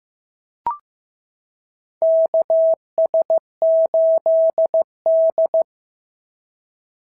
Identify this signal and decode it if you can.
Morse code, heard as a keyed tone, about 14 wpm, 650 Hz. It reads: KS8D
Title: SOTA QSO Practice 15wpm